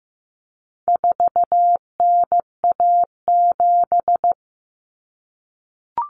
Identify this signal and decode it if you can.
Morse code, heard as a keyed tone, about 15 wpm, 700 Hz. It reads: 4NA7
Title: Common Cut Numbers - 4-Digit 15wpm